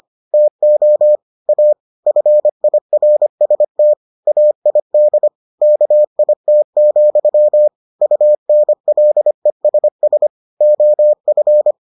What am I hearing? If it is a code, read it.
Morse code, heard as a keyed tone, about 25 wpm, 600 Hz. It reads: TO A FIRST AID KIT, UNLESS OF